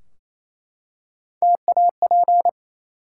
Morse code, heard as a keyed tone, about 28 wpm, 700 Hz. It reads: TAP
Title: Three Letter Words - 28wpm